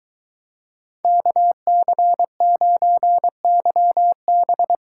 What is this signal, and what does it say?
Morse code, heard as a keyed tone, about 23 wpm, 700 Hz. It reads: KC9YB